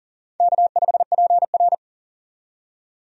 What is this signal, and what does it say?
Morse code, heard as a keyed tone, about 40 wpm, 700 Hz. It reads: K5PR
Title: USA Call Signs - 4 Characters - 40wpm